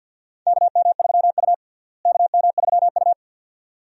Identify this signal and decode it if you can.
Morse code, heard as a keyed tone, about 48 wpm, 700 Hz. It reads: KM3U KM3U
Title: USA Call Signs - 4 Characters - Repeated 2x - 50wpm